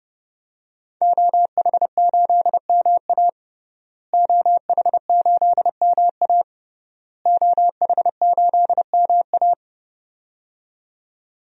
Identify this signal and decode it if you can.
Morse code, heard as a keyed tone, about 30 wpm, 700 Hz. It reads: OH8MA OH8MA OH8MA
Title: DXCC-100 Call Signs - 5 Characters - Repeated 3x - 30wpm